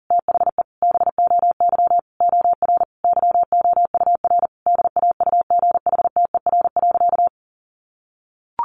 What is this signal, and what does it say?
Morse code, heard as a keyed tone, about 37 wpm, 700 Hz. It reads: THE BOY OR YOUR DAUGHTER.
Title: Sentences from Top 1500 Words 40wpm